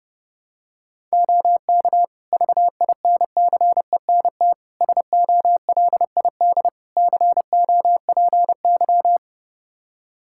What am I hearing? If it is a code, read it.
Morse code, heard as a keyed tone, about 31 wpm, 700 Hz. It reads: OK VINCENT SOLID COPY
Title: Practice QSO - Exchange 3 - More Info - 30wpm